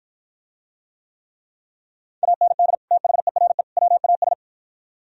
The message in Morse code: AND THERE WAS

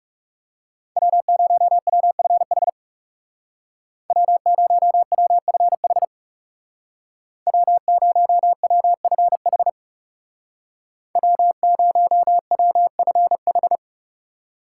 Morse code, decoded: W0WFH W0WFH W0WFH W0WFH

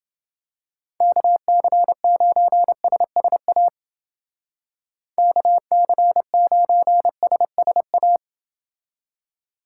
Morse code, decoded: KC9SSA KC9SSA

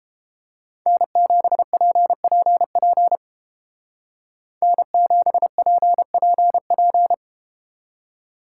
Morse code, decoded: N7PPP N7PPP

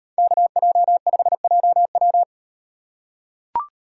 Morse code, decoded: KJ5JW

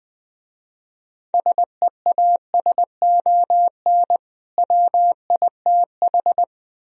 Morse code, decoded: SEASON WITH

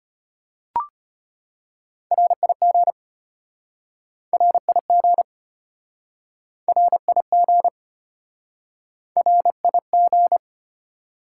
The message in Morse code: RIG RIG RIG RIG

